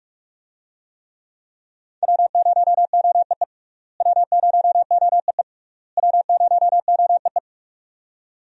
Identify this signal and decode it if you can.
Morse code, heard as a keyed tone, about 43 wpm, 700 Hz. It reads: W0OEE W0OEE W0OEE
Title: USA Call Signs - 5 Characters - Repeated 3x - 45wpm